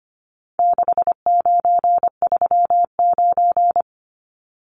Morse code, heard as a keyed tone, about 25 wpm, 700 Hz. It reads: 6939
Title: Numbers - 4-Digit 25wpm